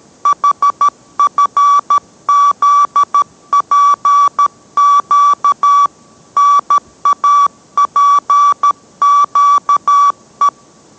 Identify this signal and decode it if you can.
Morse code, heard as a keyed tone, about 13 wpm, 1150 Hz. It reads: HFZPQ NAPQE